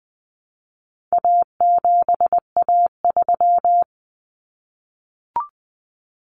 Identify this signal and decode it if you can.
Morse code, heard as a keyed tone, about 20 wpm, 700 Hz. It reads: A7A3
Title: Common Cut Numbers - 4-Digit 20wpm